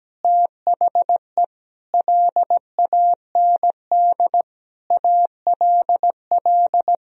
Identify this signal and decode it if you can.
Morse code, heard as a keyed tone, about 17 wpm, 700 Hz. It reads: THE LAND ALL
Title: Sentences from Top 1000 Words - Review Part 1 - 17wpm